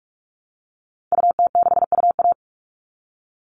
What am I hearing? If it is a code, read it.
Morse code, heard as a keyed tone, about 44 wpm, 700 Hz. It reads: UT6UA
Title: Call Signs Repeated 2x - 45wpm